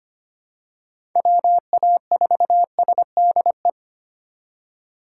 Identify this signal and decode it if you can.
Morse code, heard as a keyed tone, about 25 wpm, 700 Hz. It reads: WA4SDE
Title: USA Call Signs - 6 Characters - 2x Character Spacing - 25wpm